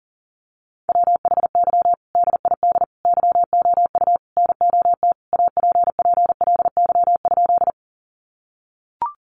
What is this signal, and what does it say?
Morse code, heard as a keyed tone, about 40 wpm, 700 Hz. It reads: WHY DID YOU NOT APPLY?
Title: Sentences from Top 1200 Words 40wpm